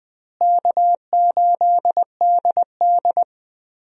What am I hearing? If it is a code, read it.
Morse code, heard as a keyed tone, about 20 wpm, 700 Hz. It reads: K8DD
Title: USA Call Signs - 4 Characters - Repeated 3x - 20wpm